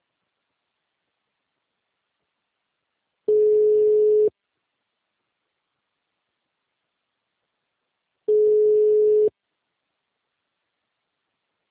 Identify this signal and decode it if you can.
Ringback tone (Europe, 425 Hz).
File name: out- -2008- - - .17385